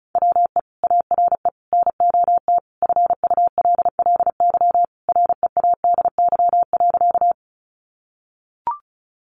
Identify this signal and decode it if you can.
Morse code, heard as a keyed tone, about 34 wpm, 700 Hz. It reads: WE ARE NOT FULLY READY.
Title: Sentences from Top 1300 Words 35wpm